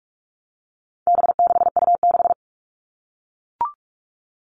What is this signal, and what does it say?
Morse code, heard as a keyed tone, about 44 wpm, 700 Hz. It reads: B6U6
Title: All Cut Numbers - 4-Digit 45wpm